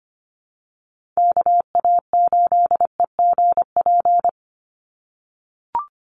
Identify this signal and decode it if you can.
Morse code, heard as a keyed tone, about 24 wpm, 700 Hz. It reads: KA8EGP
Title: USA Call Signs 25wpm